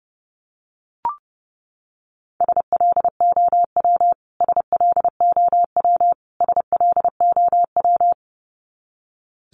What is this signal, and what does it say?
Morse code, heard as a keyed tone, about 30 wpm, 700 Hz. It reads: SLOW SLOW SLOW
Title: Top 500 Words Repeated 30wpm